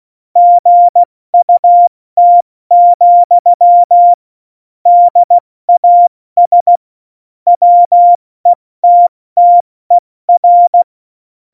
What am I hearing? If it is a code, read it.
Morse code, heard as a keyed tone, about 16 wpm, 700 Hz. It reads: GUT, DAS WETTER
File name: Morsekurs / Witze / Witz 05. Januar 2021, 70 F80